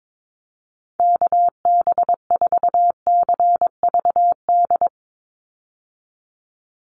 Morse code, heard as a keyed tone, about 22 wpm, 700 Hz. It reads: KB4CVD
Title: USA Call Signs 22wpm